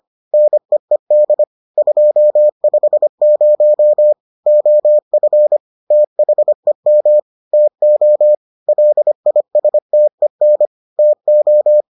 Morse code, heard as a keyed tone, about 25 wpm, 600 Hz. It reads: NEED 250 OF THEM TO LISTEN TO